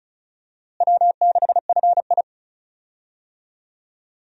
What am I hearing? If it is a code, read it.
Morse code, heard as a keyed tone, about 34 wpm, 700 Hz. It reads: W6FI
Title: USA Call Signs - 4 Characters - Repeated 3x - 35wpm